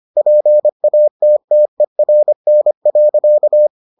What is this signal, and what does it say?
Morse code, heard as a keyed tone, about 25 wpm, 600 Hz. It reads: PATTERN.